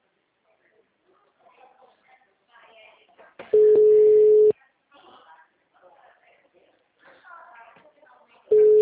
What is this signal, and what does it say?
Ringback tone (Europe, 425 Hz), with speech.
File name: out- -2010- - - .44